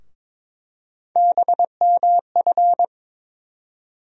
Morse code, heard as a keyed tone, about 22 wpm, 700 Hz. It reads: BMF
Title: USA Call Sign Postfixes - 3 Letters 22wpm